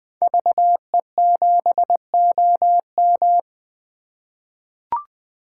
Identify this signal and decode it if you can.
Morse code, heard as a keyed tone, about 21 wpm, 700 Hz. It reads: VE7OM
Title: 2023 SOTA Chaser Call Signs - 20wpm